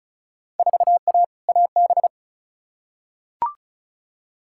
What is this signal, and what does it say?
Morse code, heard as a keyed tone, about 34 wpm, 700 Hz. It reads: 4A AB
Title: Field Day Exchanges Encore - 35wpm